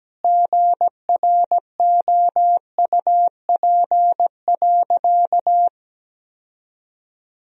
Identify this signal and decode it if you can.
Morse code, heard as a keyed tone, about 17 wpm, 700 Hz. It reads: GROUP.
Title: Sentences from Top 1000 Words - Review Part 1 - 17wpm